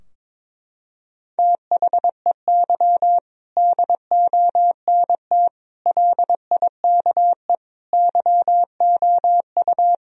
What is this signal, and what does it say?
Morse code, heard as a keyed tone, about 22 wpm, 700 Hz. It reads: THEY DONT LIKE YOU
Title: Sentences from Top 1000 Words - Review Part 2 - 22wpm